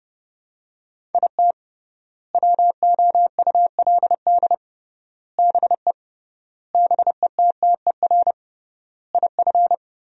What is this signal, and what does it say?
Morse code, heard as a keyed tone, about 30 wpm, 700 Hz. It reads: IT WOULD BE BETTER IF
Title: 3x Word-Spacing - Sentences from Top 400 Words - 30wpm